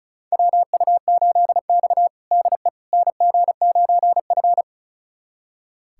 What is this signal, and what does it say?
Morse code, heard as a keyed tone, about 36 wpm, 700 Hz. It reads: WU8X DE NG9F
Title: USA Callsign DE Callsign - 4 Characters - 35wpm